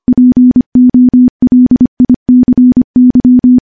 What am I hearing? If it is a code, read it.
Morse code, heard as a keyed tone, about 25 wpm, 250 Hz. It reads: POLICY
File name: morse-training-25WPM